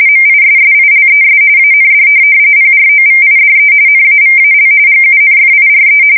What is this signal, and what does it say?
RTTY, about 45.45 baud. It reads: MISSILES AND PROVIDE NAVAL GUNFIRE
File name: fsk170 / USS-Missouri